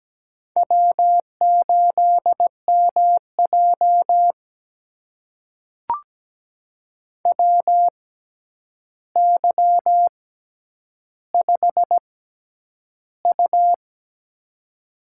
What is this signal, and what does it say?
Morse code, heard as a keyed tone, about 17 wpm, spaced as for 9 wpm, 700 Hz. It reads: W8MJ WY5U
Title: USA Call Signs - 4 Characters - 6x Character Spacing - 17wpm